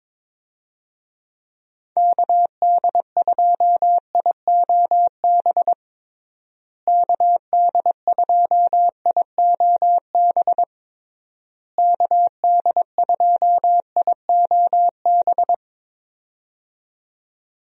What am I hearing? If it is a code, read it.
Morse code, heard as a keyed tone, about 22 wpm, 700 Hz. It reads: KD2IOB KD2IOB KD2IOB
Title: USA Call Signs - 6 Characters - Repeated 3x - 22wpm